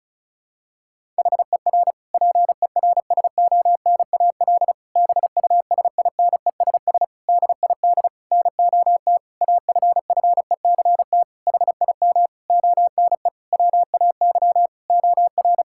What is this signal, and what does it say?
Morse code, heard as a keyed tone, about 36 wpm, 700 Hz. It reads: HER PERSONAL BUSINESS DID NOT AFFECT HIM ONE WAY OR
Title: Sentences from Top 1000 Words - Review Part 6 - 35wpm